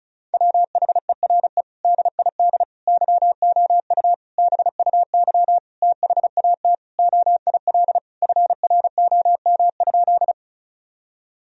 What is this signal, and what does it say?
Morse code, heard as a keyed tone, about 34 wpm, 700 Hz. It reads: WHERE DID YOU BUY THAT OIL FROM?
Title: Sentences from Top 1000 Words - Review Part 4 - 35wpm